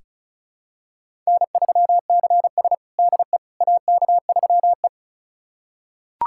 Morse code, heard as a keyed tone, about 36 wpm, 700 Hz. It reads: N3CS DE AK3E
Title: USA Callsign DE Callsign - 4 Characters - 35wpm